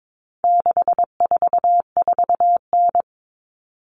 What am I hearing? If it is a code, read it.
Morse code, heard as a keyed tone, about 22 wpm, 700 Hz. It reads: 644N
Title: All Cut Numbers - 4-Digit 22wpm